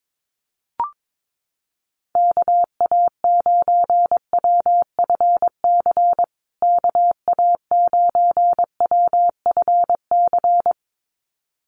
Morse code, heard as a keyed tone, about 22 wpm, 700 Hz. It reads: KA9WFC KA9WFC
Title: USA Call Signs Repeated 2x 22wpm